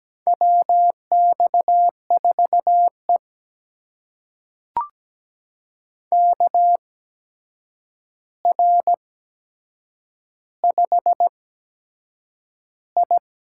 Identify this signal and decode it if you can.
Morse code, heard as a keyed tone, about 17 wpm, spaced as for 7 wpm, 700 Hz. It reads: WX4E KR5I
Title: USA Call Signs - 4 Characters - 8x Character Spacing - 17wpm